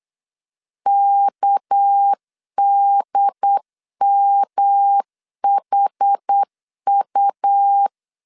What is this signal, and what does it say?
Morse code, heard as a keyed tone, about 8 wpm, 800 Hz. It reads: KDMHU
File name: LEZIONI / 6 PKX0 / 6 PK 8wpm 3003